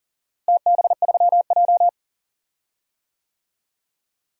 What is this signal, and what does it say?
Morse code, heard as a keyed tone, about 40 wpm, 700 Hz. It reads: TB3J
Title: DXCC-100 Call Signs - 4 Characters - Repeated 2x - 40wpm